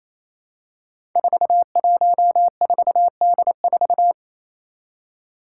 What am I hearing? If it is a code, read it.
Morse code, heard as a keyed tone, about 28 wpm, 700 Hz. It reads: 414D4
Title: Five Challenging Letter-Number Combinations 28wpm